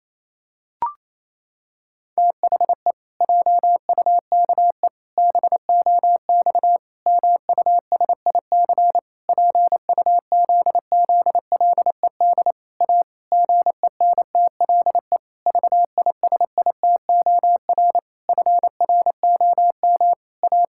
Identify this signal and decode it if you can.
Morse code, heard as a keyed tone, about 28 wpm, 700 Hz. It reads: THE JUKE BOX MUSIC PUZZLED A GENTLE VISITOR FROM A
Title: Pangram Challenge v2 - 28wpm